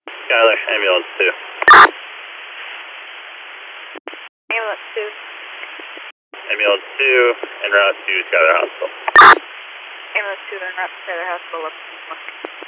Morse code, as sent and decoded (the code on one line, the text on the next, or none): none